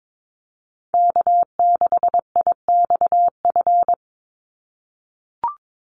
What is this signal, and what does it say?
Morse code, heard as a keyed tone, about 22 wpm, 700 Hz. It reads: K6IXF
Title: USA Call Signs 22wpm